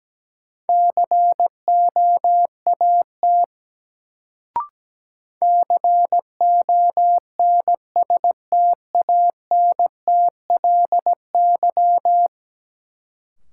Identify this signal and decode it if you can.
Morse code, heard as a keyed tone, about 17 wpm, 700 Hz. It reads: COAT CONSTANTLY
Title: Sentences from Top 1600 Words - New Word Drill Alt - 17wpm